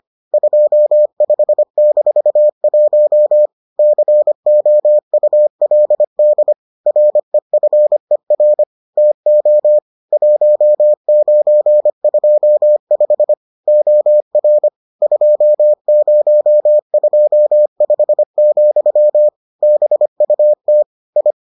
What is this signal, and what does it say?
Morse code, heard as a keyed tone, about 25 wpm, 600 Hz. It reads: 25-1 COULD REFER TO 1925 OR 2025, BUT I